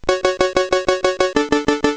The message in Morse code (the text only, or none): none